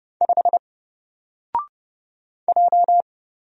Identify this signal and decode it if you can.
Morse code, heard as a keyed tone, about 29 wpm, 700 Hz. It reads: 5 J
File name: Single-Letter-Number-30wpm